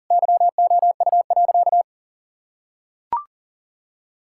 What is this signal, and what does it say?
Morse code, heard as a keyed tone, about 40 wpm, 700 Hz. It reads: YOU.
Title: Sentences from Top 1600 Words 40wpm